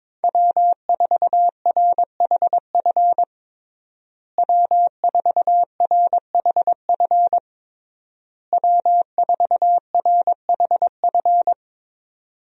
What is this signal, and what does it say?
Morse code, heard as a keyed tone, about 22 wpm, 700 Hz. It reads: W4RHF W4RHF W4RHF
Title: USA Call Signs - 5 Characters - Repeated 3x - 22wpm